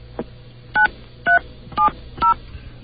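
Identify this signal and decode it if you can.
DTMF, heard as telephone touch tones, tones of about 120 ms. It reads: 9370
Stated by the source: Off The Hook, 2600.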